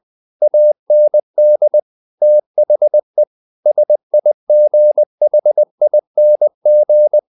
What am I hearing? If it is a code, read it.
Morse code, heard as a keyed tone, about 20 wpm, 600 Hz. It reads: AND THE SIGHING
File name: the old man and the sea / the old man and the sea part 1 20wpm